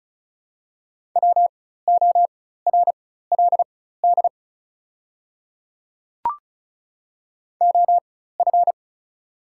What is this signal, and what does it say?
Morse code, heard as a keyed tone, about 36 wpm, spaced as for 16 wpm, 700 Hz. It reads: WORLD OF
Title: 4x Character Spacing - Top 200 Words - 35wpm